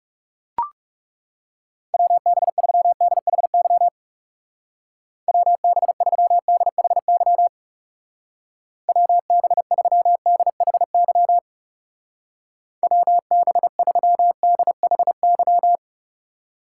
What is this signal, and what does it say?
Morse code, heard as a keyed tone, about 35 wpm, 700 Hz. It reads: WB3DHY WB3DHY WB3DHY WB3DHY
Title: Speed Racing - USA Call Signs 30wpm